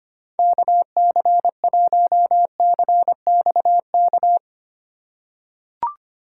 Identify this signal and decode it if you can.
Morse code, heard as a keyed tone, about 25 wpm, 700 Hz. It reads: KC1CXK
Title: USA Call Signs - 6 Characters - 3x Character Spacing - 25wpm